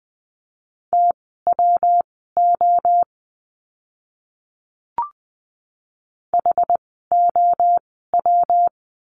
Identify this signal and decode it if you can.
Morse code, heard as a keyed tone, about 20 wpm, spaced as for 14 wpm, 700 Hz. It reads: TWO HOW
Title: 237 Lesson 38f - Letter q - Words 20wpm